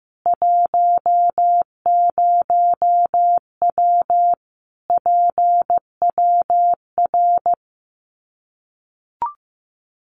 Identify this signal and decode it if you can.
Morse code, heard as a keyed tone, about 15 wpm, 700 Hz. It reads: 10W PWR
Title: Common QSO Phrases 15wpm